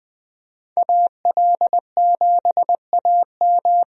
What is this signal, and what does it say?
Morse code, heard as a keyed tone, about 20 wpm, 700 Hz. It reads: AL7AM
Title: USA Call Signs - 5 Characters - Repeated 2x - 20wpm